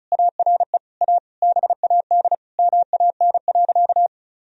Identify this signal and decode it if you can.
Morse code, heard as a keyed tone, about 34 wpm, 700 Hz. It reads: ARE A BAD MAN.
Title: Sentences from Top 1000 Words - Review Part 1 - 35wpm